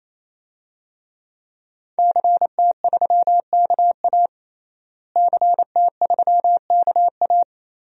Morse code, heard as a keyed tone, about 28 wpm, 700 Hz. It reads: CT3KA CT3KA
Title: DXCC-100 Call Signs - 5 Characters - Repeated 3x - 28wpm